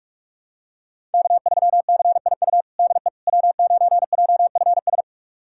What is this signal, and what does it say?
Morse code, heard as a keyed tone, about 47 wpm, 700 Hz. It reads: K3CIU DE W9JFS
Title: USA Callsign DE Callsign - 5 Characters - 45wpm